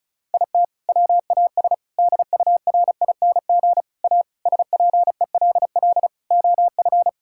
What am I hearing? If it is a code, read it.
Morse code, heard as a keyed tone, about 34 wpm, 700 Hz. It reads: IT WAS DURING A SPELL OF
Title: Sentences from Top 1000 Words - Review Part 5 - 35wpm